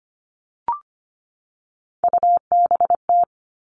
Morse code, heard as a keyed tone, about 25 wpm, 700 Hz. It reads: UBT
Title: All Cut Numbers - 3-Digit 25wpm